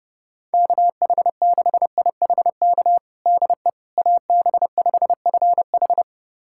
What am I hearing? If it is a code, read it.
Morse code, heard as a keyed tone, about 30 wpm, 700 Hz. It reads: KH6IHK DE AB5FH
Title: USA Callsign DE Callsign 30wpm